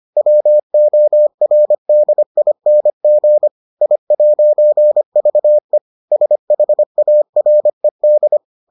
Morse code, heard as a keyed tone, about 25 wpm, 600 Hz. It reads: WORDING I'VE SHARED